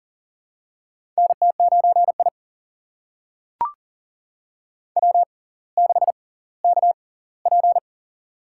Morse code, heard as a keyed tone, about 38 wpm, spaced as for 20 wpm, 700 Hz. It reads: NT9I W6KP
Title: USA Call Signs - 6x Character Spacing - 40wpm